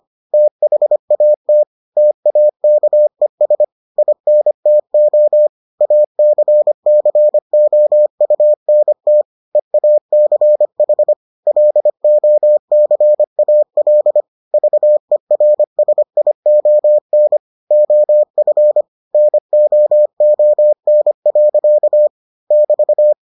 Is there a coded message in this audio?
Yes: Morse code from 0.3 to 23.1 s, heard as a keyed tone, about 25 wpm, 600 Hz. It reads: THAT TAKES INTO ACCOUNT EACH LOCAL VERSION OF NOON. <BT>